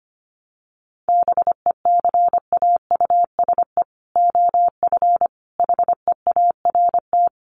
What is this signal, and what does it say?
Morse code, heard as a keyed tone, about 25 wpm, 700 Hz. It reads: BECAUSE OF HEART
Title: Sentences from Top 1200 Words 25wpm